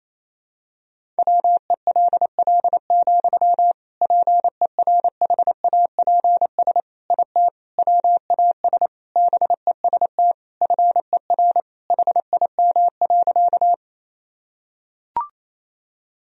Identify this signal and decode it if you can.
Morse code, heard as a keyed tone, about 28 wpm, 700 Hz. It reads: WELL, PERHAPS IT WAS BEST FER HIM.
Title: Sentences from Top 500 Words and Abbreviations 28wpm